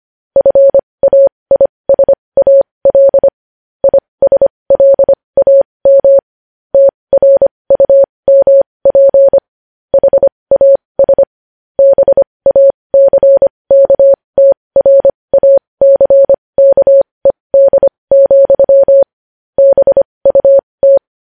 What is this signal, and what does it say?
Morse code, heard as a keyed tone, about 25 wpm, spaced as for 20 wpm, 550 Hz. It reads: FAISAL ISLAM TRUMP HAS BACKTRACKED, BUT
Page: News Headlines in Morse Code at 20 WPM